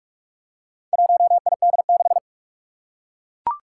Morse code, heard as a keyed tone, about 43 wpm, 700 Hz. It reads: 1ID6
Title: Four Challenging Letter-Number Combinations 45wpm